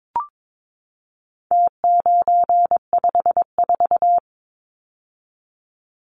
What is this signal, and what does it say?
Morse code, heard as a keyed tone, about 22 wpm, 700 Hz. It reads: T954
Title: Numbers - Contest Style - 4-Digit 22wpm